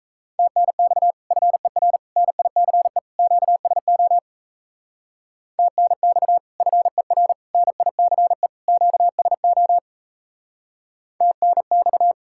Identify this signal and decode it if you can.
Morse code, heard as a keyed tone, about 35 wpm, 700 Hz. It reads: TNX FER NICE QSO TNX FER NICE QSO TNX
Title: Speed-Racing Common QSO Phrases - 28wpm